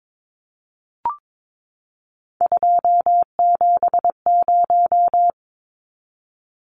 Morse code, heard as a keyed tone, about 22 wpm, 700 Hz. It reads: 270
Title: Numbers - 3-Digit 22wpm